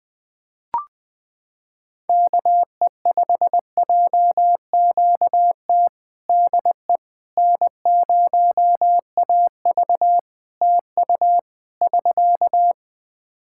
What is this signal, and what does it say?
Morse code, heard as a keyed tone, about 21 wpm, 700 Hz. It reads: KE5JQT DE N0AV TU <SK>
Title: Practice QSO - Exchange 4 - Ending - 20wpm